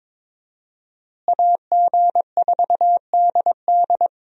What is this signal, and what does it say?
Morse code, heard as a keyed tone, about 22 wpm, 700 Hz. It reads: AG4DD
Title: USA Call Signs - 5 Characters - Repeated 3x - 22wpm